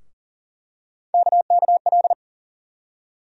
Morse code, heard as a keyed tone, about 40 wpm, 700 Hz. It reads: KKL
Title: USA Call Sign Postfixes - 3 Letters 40wpm